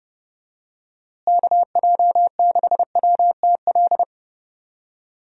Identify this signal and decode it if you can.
Morse code, heard as a keyed tone, about 32 wpm, 700 Hz. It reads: KJ6WTL
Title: USA Call Signs - 6 Characters - Repeated 3x - 30wpm